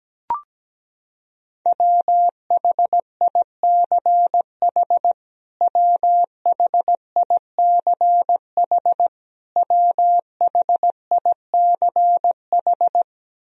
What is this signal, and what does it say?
Morse code, heard as a keyed tone, about 17 wpm, 700 Hz. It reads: WHICH WHICH WHICH
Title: Top 100 Words Repeated 17wpm